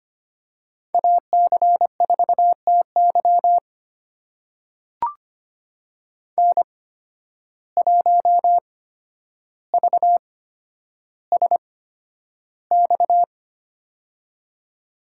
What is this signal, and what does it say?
Morse code, heard as a keyed tone, about 25 wpm, spaced as for 11 wpm, 700 Hz. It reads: AC4TY N1VSX
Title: USA Call Signs - 5 Characters - 8x Character Spacing - 25wpm